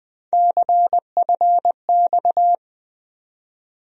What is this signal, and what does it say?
Morse code, heard as a keyed tone, about 20 wpm, 700 Hz. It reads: CFX
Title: USA Call Sign Postfixes - 3 Letters 20wpm